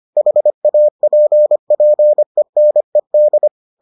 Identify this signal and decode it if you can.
Morse code, heard as a keyed tone, about 25 wpm, 600 Hz. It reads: HAPPENED